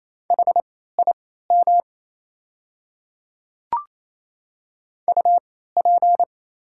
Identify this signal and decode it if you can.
Morse code, heard as a keyed tone, about 27 wpm, spaced as for 16 wpm, 690 Hz. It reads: HIM UP
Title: 3x Character Spacing - Top 200 Words - 28wpm